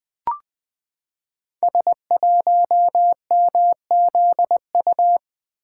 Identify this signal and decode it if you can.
Morse code, heard as a keyed tone, about 20 wpm, 700 Hz. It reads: S1MZU
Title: Code Groups - 5 Letter:Numbers - 20wpm